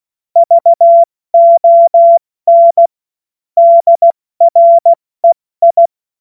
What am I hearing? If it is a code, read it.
Morse code, heard as a keyed tone, about 16 wpm, 680 Hz. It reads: VON DREI
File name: Morsekurs / Witze / Witz 25. Oktober 2022 70 F80